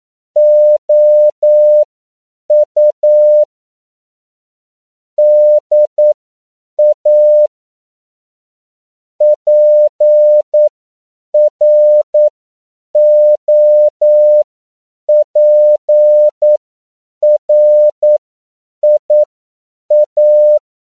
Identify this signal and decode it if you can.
Morse code, heard as a keyed tone, about 9 wpm, spaced as for 7 wpm, 600 Hz. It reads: OU DA PROPRIA
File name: CW / 10PPM